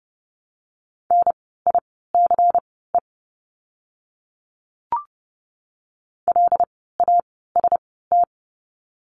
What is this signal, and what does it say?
Morse code, heard as a keyed tone, about 30 wpm, spaced as for 17 wpm, 700 Hz. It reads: NICE LAST
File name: Lesson 16f - Letter c - Words 30wpm